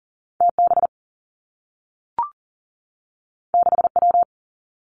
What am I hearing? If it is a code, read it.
Morse code, heard as a keyed tone, about 40 wpm, 700 Hz. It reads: TB 6W
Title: Two Challenging Letter-Number Combinations 40wpm